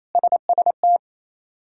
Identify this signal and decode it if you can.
Morse code, heard as a keyed tone, about 28 wpm, 700 Hz. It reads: SST
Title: K1USN SST QSO Practice - 28wpm